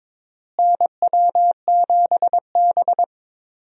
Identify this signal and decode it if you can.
Morse code, heard as a keyed tone, about 22 wpm, 700 Hz. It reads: NW7B
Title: USA Call Signs - 4 Characters - Repeated 2x - 22wpm